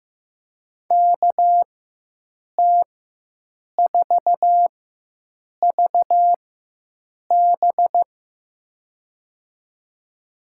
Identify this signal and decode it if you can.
Morse code, heard as a keyed tone, about 16 wpm, spaced as for 7 wpm, 700 Hz. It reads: KT4VB
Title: USA Call Signs - 5 Characters - 4x Character Spacing - 15wpm